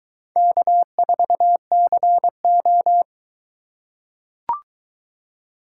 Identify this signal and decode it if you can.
Morse code, heard as a keyed tone, about 23 wpm, 700 Hz. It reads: K4CO